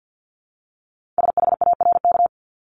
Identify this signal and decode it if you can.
Morse code, heard as a keyed tone, about 50 wpm, 700 Hz. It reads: SHARK
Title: Multi-letter Phonemes Repeated 3x 50wpm